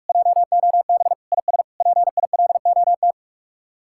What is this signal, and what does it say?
Morse code, heard as a keyed tone, about 45 wpm, 700 Hz. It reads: JOB IS PILOT